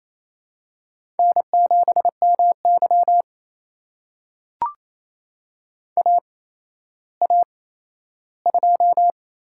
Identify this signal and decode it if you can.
Morse code, heard as a keyed tone, about 28 wpm, spaced as for 14 wpm, 700 Hz. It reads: N7MY AA2
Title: USA Call Signs - 4 Characters - 8x Character Spacing - 28wpm